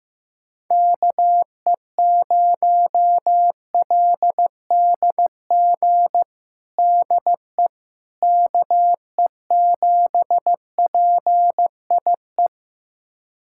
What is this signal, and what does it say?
Morse code, heard as a keyed tone, about 15 wpm, 700 Hz. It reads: KE0LDG DE KE7PIE
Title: USA Callsign DE Callsign - 6 Characters - 15wpm